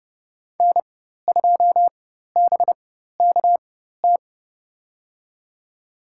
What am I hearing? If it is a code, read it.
Morse code, heard as a keyed tone, about 32 wpm, spaced as for 14 wpm, 700 Hz. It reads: N2BKT
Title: USA Call Signs - 5 Characters - 4x Character Spacing - 30wpm